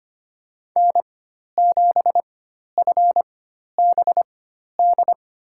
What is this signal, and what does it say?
Morse code, heard as a keyed tone, about 25 wpm, spaced as for 12 wpm, 700 Hz. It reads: N7FBD
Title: USA Call Signs - 5 Characters - 4x Character Spacing - 25wpm